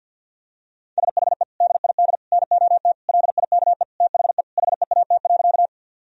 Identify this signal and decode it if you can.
Morse code, heard as a keyed tone, about 50 wpm, 700 Hz. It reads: SHE DID NOT LIKE THE HEAT.
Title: Sentences from Top 1000 Words - Review Part 3 - 50wpm